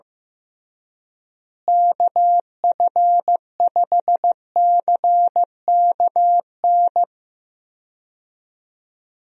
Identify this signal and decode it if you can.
Morse code, heard as a keyed tone, about 15 wpm, 700 Hz. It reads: KF5CKN
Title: USA Call Signs - 6 Characters - Repeated 2x - 15wpm